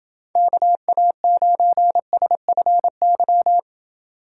Morse code, heard as a keyed tone, about 29 wpm, 700 Hz. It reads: KA9SFY